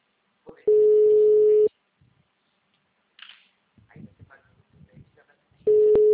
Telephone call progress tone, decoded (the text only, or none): ringback tone